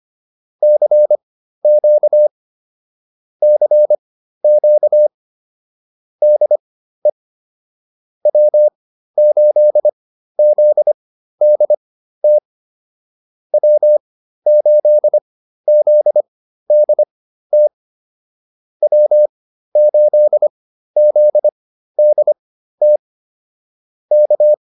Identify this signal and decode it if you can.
Morse code, heard as a keyed tone, about 25 wpm, spaced as for 13 wpm, 600 Hz. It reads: CQ CQ DE W8ZDT W8ZDT W8ZDT K
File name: QSO 108 13